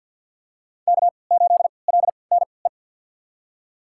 Morse code, heard as a keyed tone, about 46 wpm, spaced as for 27 wpm, 700 Hz. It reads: K8LNE
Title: USA Call Signs - 3x Character Spacing - 50wpm